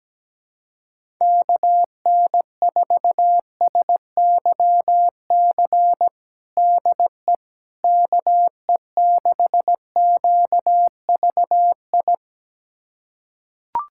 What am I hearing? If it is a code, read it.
Morse code, heard as a keyed tone, about 17 wpm, 700 Hz. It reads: KN4SYC DE KE6QVI
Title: USA Callsign DE Callsign - 6 Characters - 17wpm